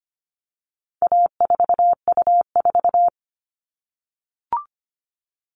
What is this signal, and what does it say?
Morse code, heard as a keyed tone, about 25 wpm, 700 Hz. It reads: A4U4
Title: All Cut Numbers - 4-Digit 25wpm